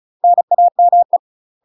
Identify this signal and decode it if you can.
Morse code, heard as a keyed tone, about 34 wpm, 700 Hz. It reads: NAME
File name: UK MASTER QSOs TRAINING / 35WPM / UK MASTER QSOs 0023